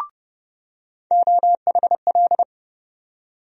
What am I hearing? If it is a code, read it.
Morse code, heard as a keyed tone, about 30 wpm, 700 Hz. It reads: OHL
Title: USA Call Sign Postfixes - 3 Letters 30wpm